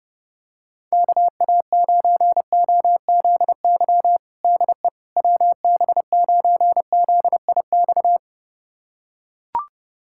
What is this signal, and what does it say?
Morse code, heard as a keyed tone, about 28 wpm, 700 Hz. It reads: KA9OZY DE WB9ZIX
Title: USA Callsign DE Callsign - 6 Characters - 30wpm